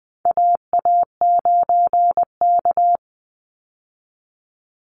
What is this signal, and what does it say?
Morse code, heard as a keyed tone, about 20 wpm, 700 Hz. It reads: AA9K
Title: Call Signs Repeated 3x - 20wpm